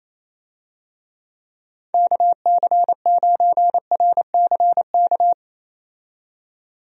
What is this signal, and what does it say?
Morse code, heard as a keyed tone, about 27 wpm, 700 Hz. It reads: KC9RCK
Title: USA Call Signs - 6 Characters - 28wpm